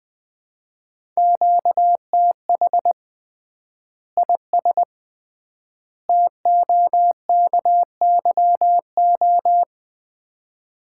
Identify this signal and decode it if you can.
Morse code, heard as a keyed tone, about 20 wpm, 700 Hz. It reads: QTH IS TOKYO
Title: Common QSO Phrases - 3x Word-Spacing - 20wpm